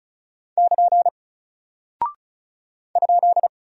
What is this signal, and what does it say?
Morse code, heard as a keyed tone, about 36 wpm, 700 Hz. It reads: <KN> ?
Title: Single Character (Letter:Number:Punctuation:Prosign) Alt 35wpm